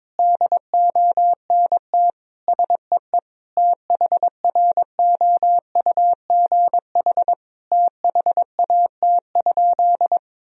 Morse code, heard as a keyed tone, about 22 wpm, 700 Hz. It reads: DONT SEE THROUGH THAT?
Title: Sentences from Top 1000 Words - Review Part 1 - 22wpm